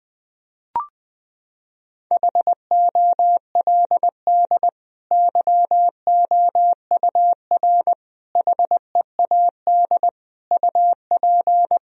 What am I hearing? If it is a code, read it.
Morse code, heard as a keyed tone, about 20 wpm, 700 Hz. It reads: HOLD YOUR HEAD UP
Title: Sentences from Top 1000 Words - Review Part 2 - 20wpm